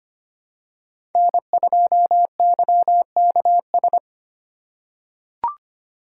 Morse code, heard as a keyed tone, about 25 wpm, 700 Hz. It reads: N2YKS
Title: USA Call Signs - 5 Characters - 3x Character Spacing - 25wpm